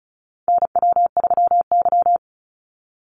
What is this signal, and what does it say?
Morse code, heard as a keyed tone, about 34 wpm, 700 Hz. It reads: NW3Y
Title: Call Signs - 35wpm